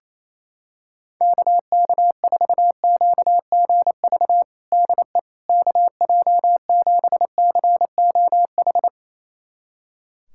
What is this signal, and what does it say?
Morse code, heard as a keyed tone, about 28 wpm, 700 Hz. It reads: KK4QGV DE KJ7COH
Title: USA Callsign DE Callsign 28wpm